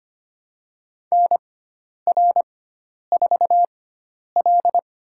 Morse code, heard as a keyed tone, about 25 wpm, spaced as for 10 wpm, 700 Hz. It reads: NR4L
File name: Lesson 16g - Letter c - Call Signs 25wpm